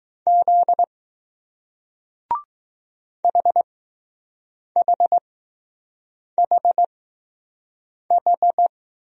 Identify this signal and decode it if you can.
Morse code, heard as a keyed tone, about 20 wpm, spaced as for 6 wpm, 700 Hz. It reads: Z HHHH